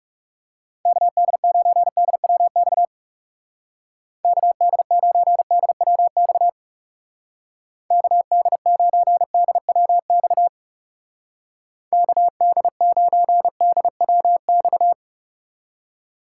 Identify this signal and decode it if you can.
Morse code, heard as a keyed tone, about 35 wpm, 700 Hz. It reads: KD9DWX KD9DWX KD9DWX KD9DWX
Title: Speed Racing - USA 2x3 Call Signs 30wpm